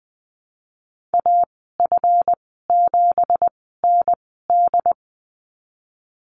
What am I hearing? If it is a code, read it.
Morse code, heard as a keyed tone, about 19 wpm, spaced as for 14 wpm, 700 Hz. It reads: AF7ND
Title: 167 Lesson 27g - Number 7 - Call Signs 20wpm